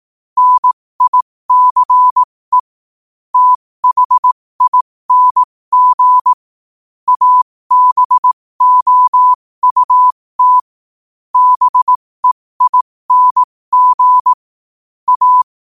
Morse code, heard as a keyed tone, about 18 wpm, spaced as for 15 wpm, 1000 Hz. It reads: NICE THING ABOUT BEING A
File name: qotdfiles / 15wpm